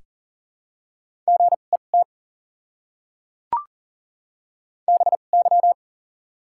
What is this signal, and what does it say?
Morse code, heard as a keyed tone, about 39 wpm, 700 Hz. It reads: G E T B Y
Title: 2x Character Spacing - Top 500 Words - 40wpm